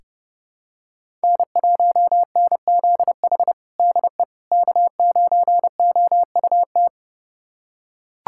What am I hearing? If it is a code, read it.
Morse code, heard as a keyed tone, about 30 wpm, 700 Hz. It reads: N1NZH DE K9OUT
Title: USA Callsign DE Callsign - 5 Characters - 30wpm